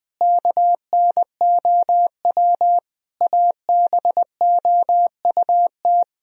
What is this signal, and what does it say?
Morse code, heard as a keyed tone, about 20 wpm, 700 Hz. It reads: KNOW ABOUT
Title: Sentences from Top 1000 Words - Review Part 4 - 20wpm